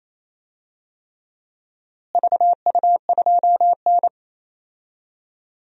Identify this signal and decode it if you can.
Morse code, heard as a keyed tone, about 28 wpm, 700 Hz. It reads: VU2N
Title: DXCC-100 Call Signs - 4 Characters - Repeated 3x - 28wpm